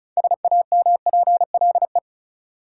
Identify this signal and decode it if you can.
Morse code, heard as a keyed tone, about 34 wpm, 700 Hz. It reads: SAMPLE
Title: Six Letter Words - 35wpm